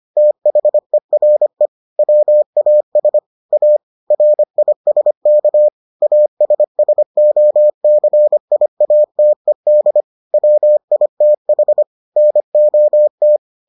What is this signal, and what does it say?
Morse code, heard as a keyed tone, about 25 wpm, 600 Hz. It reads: THERE WAS A RISK ASSOCIATED WITH NOT